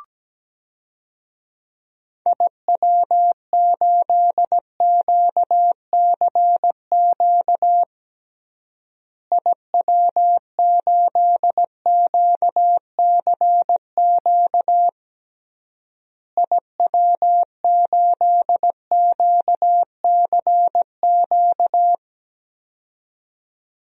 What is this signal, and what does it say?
Morse code, heard as a keyed tone, about 17 wpm, 700 Hz. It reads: IW8QCQ IW8QCQ IW8QCQ
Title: DXCC-100 Call Signs - 6 Characters - Repeated 3x - 17wpm